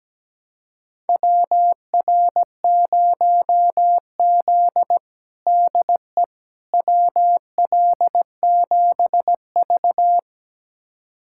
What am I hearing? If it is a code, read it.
Morse code, heard as a keyed tone, about 17 wpm, 700 Hz. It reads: WR0Z DE WL7V
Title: USA Callsign DE Callsign - 4 Characters - 17wpm